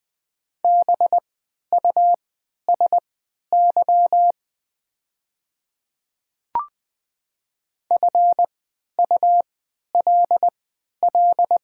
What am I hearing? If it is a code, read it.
Morse code, heard as a keyed tone, about 20 wpm, spaced as for 11 wpm, 700 Hz. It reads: BUSY FULL
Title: 3x Character Spacing - Top 500 Words - 20wpm